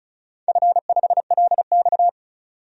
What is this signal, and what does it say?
Morse code, heard as a keyed tone, about 36 wpm, 700 Hz. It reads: F5LX
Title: DXCC-100 Call Signs - 4 Characters - 35wpm